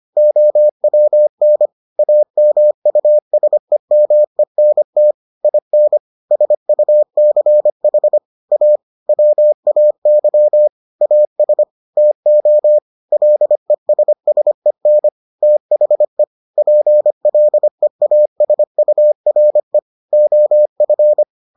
Morse code, heard as a keyed tone, about 25 wpm, 600 Hz. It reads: OWN AMUSEMENT IN SUCH A WAY AS TO LESSEN THE PLEASURE OF